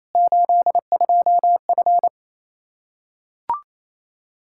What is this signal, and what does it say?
Morse code, heard as a keyed tone, about 28 wpm, 700 Hz. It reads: 82F
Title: K1USN SST QSO Practice 28wpm